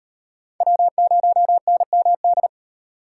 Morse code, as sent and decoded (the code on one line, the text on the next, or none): .-- ----- -. -- -..
W0NMD